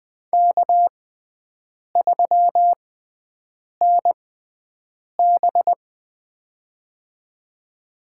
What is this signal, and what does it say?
Morse code, heard as a keyed tone, about 21 wpm, spaced as for 7 wpm, 700 Hz. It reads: K3NB
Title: USA Call Signs - 4 Characters - 6x Character Spacing - 20wpm